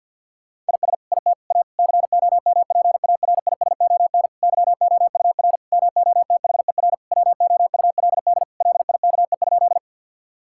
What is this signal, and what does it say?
Morse code, heard as a keyed tone, about 50 wpm, 700 Hz. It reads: IS IT A COMPARISON YOUR MOTHER WOULD LIKE?
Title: Sentences from Top 1600 Words 50wpm